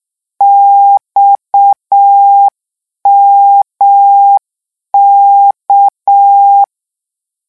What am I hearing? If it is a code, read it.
Morse code, heard as a keyed tone, about 6 wpm, 800 Hz. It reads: XMK
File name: morse mp3 / lecon 1-vitesse 6